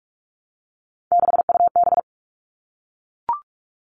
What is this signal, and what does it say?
Morse code, heard as a keyed tone, about 44 wpm, 700 Hz. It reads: 6UB
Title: All Cut Numbers - 3-Digit 45wpm